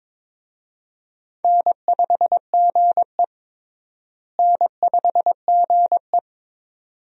Morse code, heard as a keyed tone, about 22 wpm, 700 Hz. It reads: N5GE N5GE
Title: USA Call Signs - 4 Characters - Repeated 2x - 22wpm